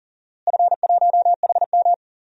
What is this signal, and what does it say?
Morse code, heard as a keyed tone, about 40 wpm, 700 Hz. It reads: F1HM
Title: DXCC-100 Call Signs - 4 Characters - Repeated 2x - 40wpm